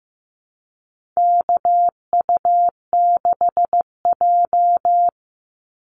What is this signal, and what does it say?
Morse code, heard as a keyed tone, about 15 wpm, 700 Hz. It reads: KU6J
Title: NA SOTA Activator Callsigns 15wpm